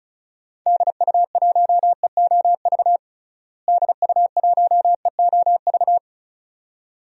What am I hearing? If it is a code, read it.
Morse code, heard as a keyed tone, about 36 wpm, 700 Hz. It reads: DU1EOV DU1EOV
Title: DXCC-100 Call Signs - 6 Characters - Repeated 2x - 35wpm